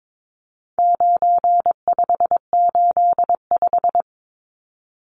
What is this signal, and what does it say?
Morse code, heard as a keyed tone, about 22 wpm, 700 Hz. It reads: 9585
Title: Numbers - 4-Digit 22wpm